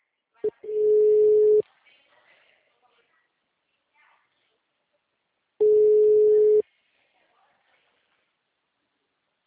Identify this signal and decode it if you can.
Ringback tone (Europe, 425 Hz), with speech.